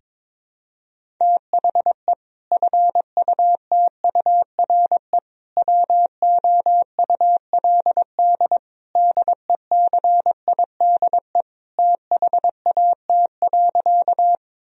Morse code, heard as a keyed tone, about 21 wpm, 700 Hz. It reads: THE FUTURE WOULD DECIDE THAT.
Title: Sentences from Top 1000 Words - Review Part 3 - 22wpm